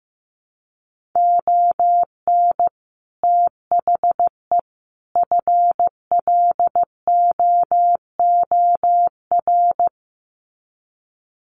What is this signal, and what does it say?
Morse code, heard as a keyed tone, about 14 wpm, 700 Hz. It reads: ON THE FLOOR
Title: Sets of 3 Words 15wpm